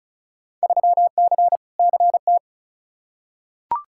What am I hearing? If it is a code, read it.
Morse code, heard as a keyed tone, about 34 wpm, 700 Hz. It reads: 3C CT
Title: Field Day Exchanges Encore - 35wpm